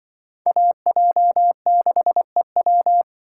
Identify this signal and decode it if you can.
Morse code, heard as a keyed tone, about 25 wpm, 700 Hz. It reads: AJ6EW
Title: Speed Racing - USA 2x2 Call Signs 20wpm